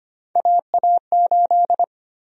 Morse code, heard as a keyed tone, about 25 wpm, 700 Hz. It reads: AA8
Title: USA Call Sign Prefixes - 25wpm